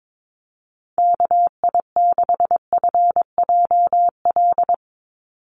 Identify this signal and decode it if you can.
Morse code, heard as a keyed tone, about 22 wpm, 700 Hz. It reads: KI6FJL
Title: USA Call Signs 22wpm